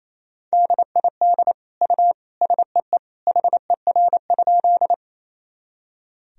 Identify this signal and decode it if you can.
Morse code, heard as a keyed tone, about 28 wpm, 700 Hz. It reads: DID U SEE HER?
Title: Sentences from Top 100 Words and Abbreviations 28wpm